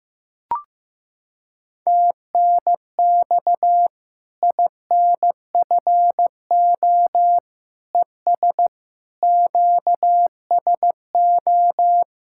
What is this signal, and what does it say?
Morse code, heard as a keyed tone, about 15 wpm, 700 Hz. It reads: TNX INFO ES QSO
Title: Practice QSO - Exchange 4 - Ending - 15wpm